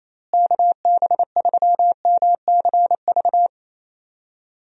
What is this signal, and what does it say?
Morse code, heard as a keyed tone, about 28 wpm, 700 Hz. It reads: KB3MCV
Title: USA Call Signs - 6 Characters - Repeated 2x - 28wpm